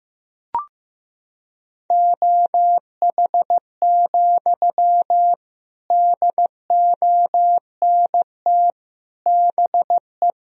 Morse code, heard as a keyed tone, about 15 wpm, 700 Hz. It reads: OH, DONT BE
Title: Sentences from Top 1000 Words - Review Part 1 - 15wpm